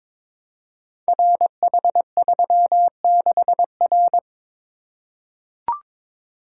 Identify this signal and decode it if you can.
Morse code, heard as a keyed tone, about 22 wpm, 700 Hz. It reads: RH36R
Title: Code Groups - 5 Letter:Numbers - 22wpm